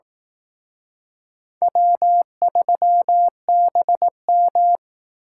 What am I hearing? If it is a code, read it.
Morse code, heard as a keyed tone, about 18 wpm, 700 Hz. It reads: W3BM
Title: Speed Racing - USA 1x2 Call Signs 15wpm